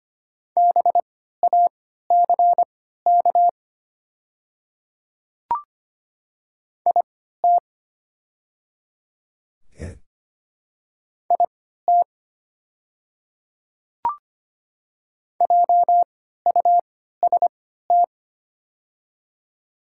Morse code, heard as a keyed tone, about 25 wpm, spaced as for 14 wpm, 700 Hz. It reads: BACK IT IT JUST
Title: 3x Character Spacing - Top 500 Words - 25wpm